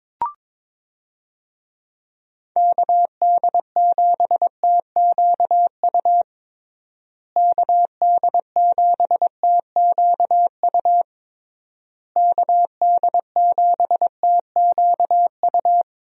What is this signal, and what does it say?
Morse code, heard as a keyed tone, about 22 wpm, 700 Hz. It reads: KD7TQU KD7TQU KD7TQU
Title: USA Call Signs - 6 Characters - Repeated 3x - 22wpm